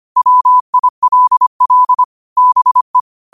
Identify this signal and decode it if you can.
Morse code, heard as a keyed tone, about 25 wpm, 1000 Hz. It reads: WILL BE
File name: qotdfiles / 25wpm